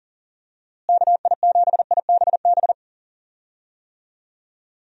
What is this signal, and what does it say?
Morse code, heard as a keyed tone, about 40 wpm, 700 Hz. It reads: KI7IBB
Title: USA Call Signs - 6 Characters - Repeated 2x - 40wpm